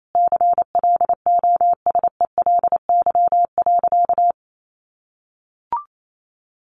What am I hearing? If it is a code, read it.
Morse code, heard as a keyed tone, about 28 wpm, 700 Hz. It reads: CLOSELY.
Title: Sentences from Top 1400 Words 28wpm